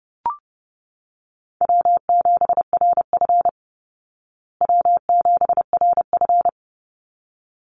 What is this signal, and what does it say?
Morse code, heard as a keyed tone, about 30 wpm, 700 Hz. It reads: W7RF W7RF
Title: Easy Call Signs Repeated 2x - 30wpm